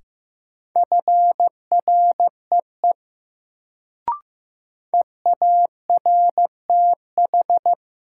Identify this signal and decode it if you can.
Morse code, heard as a keyed tone, about 15 wpm, 700 Hz. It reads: FREE EARTH
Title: Sets of 5 Words Encore - New Word Drill - 15wpm